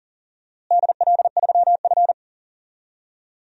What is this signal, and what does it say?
Morse code, heard as a keyed tone, about 40 wpm, 700 Hz. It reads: DL3F
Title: DXCC-100 Call Signs - 4 Characters - Repeated 3x - 40wpm